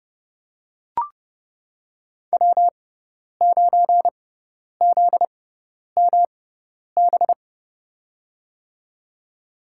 Morse code, heard as a keyed tone, about 32 wpm, spaced as for 10 wpm, 700 Hz. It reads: W9ZMB
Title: USA Call Signs - 5 Characters - 6x Character Spacing - 30wpm